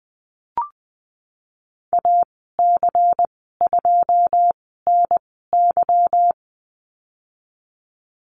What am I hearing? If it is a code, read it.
Morse code, heard as a keyed tone, about 19 wpm, spaced as for 14 wpm, 700 Hz. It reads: AC2NY
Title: 167 Lesson 27g - Number 7 - Call Signs 20wpm